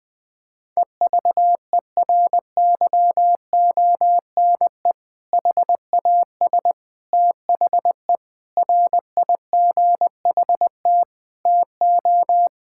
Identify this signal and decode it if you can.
Morse code, heard as a keyed tone, about 21 wpm, 700 Hz. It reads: EVERYONE HAS THE RIGHT TO
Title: Sentences with 3-Word Phrases - Spoken Beginning - 20wpm